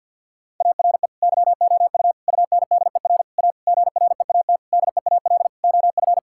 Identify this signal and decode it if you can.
Morse code, heard as a keyed tone, about 50 wpm, 700 Hz. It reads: ARE YOU UNDER A GREAT DEAL OF
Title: Sentences from Top 1000 Words - Review Part 6 - 50wpm